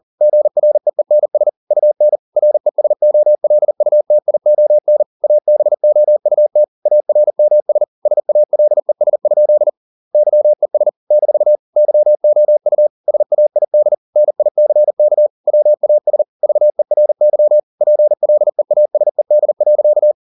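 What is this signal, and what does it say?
Morse code, heard as a keyed tone, about 40 wpm, 600 Hz. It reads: GREENS UN RESOLUTION ABOUT ARMS SALES? YES - YOU SAID NICK WAS VERY PLEASED.
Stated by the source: Morse code at 40 wpm, practice text, CW med Gnister.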